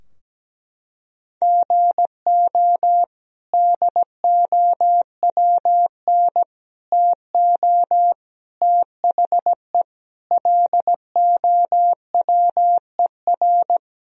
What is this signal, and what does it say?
Morse code, heard as a keyed tone, about 17 wpm, 700 Hz. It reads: GO DOWN TO THE LOWER
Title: Sentences from Top 1000 Words - Review Part 4 - 17wpm